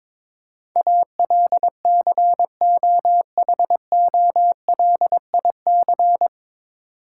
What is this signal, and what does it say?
Morse code, heard as a keyed tone, about 21 wpm, 700 Hz. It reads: ALCOHOLIC
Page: Morse Code Ninja practice 22wpm